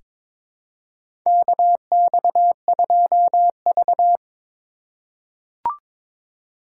Morse code, heard as a keyed tone, about 22 wpm, 700 Hz. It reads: KX2V
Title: USA Call Signs - 4 Characters - 2x Character Spacing - 22wpm